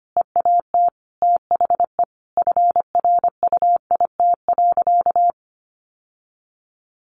Morse code, heard as a keyed tone, about 25 wpm, 700 Hz. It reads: EAT THE FRUIT.
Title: Sentences from Top 1300 Words 25wpm